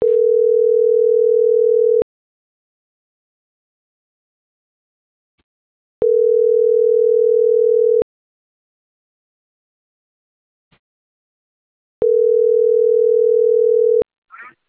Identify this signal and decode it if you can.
Ringback tone (North America), 440 + 480 Hz.